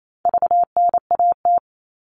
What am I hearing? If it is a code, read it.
Morse code, heard as a keyed tone, about 28 wpm, 700 Hz. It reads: VNAT
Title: All Cut Numbers - 4-Digit 28wpm